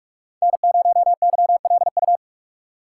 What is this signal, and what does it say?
Morse code, heard as a keyed tone, about 43 wpm, 700 Hz. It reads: N0YLU
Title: USA Call Signs - 5 Characters - 3x Character Spacing - 45wpm